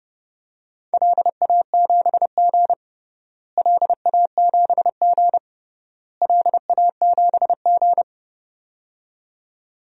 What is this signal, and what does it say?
Morse code, heard as a keyed tone, about 30 wpm, 700 Hz. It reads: LA7G LA7G LA7G
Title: DXCC-100 Call Signs - 4 Characters - Repeated 3x - 30wpm